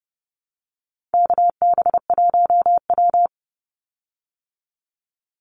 Morse code, heard as a keyed tone, about 30 wpm, 700 Hz. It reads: KB1W
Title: Call Signs Repeated 3x - 30wpm